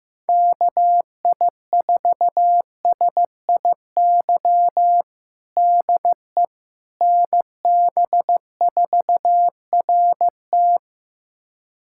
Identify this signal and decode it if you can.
Morse code, heard as a keyed tone, about 15 wpm, 700 Hz. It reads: KI4SIY DE NB4RT
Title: USA Callsign DE Callsign 15wpm